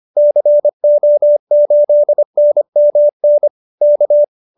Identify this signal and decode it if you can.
Morse code, heard as a keyed tone, about 25 wpm, 590 Hz. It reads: CO8NMN K
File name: QSO304 25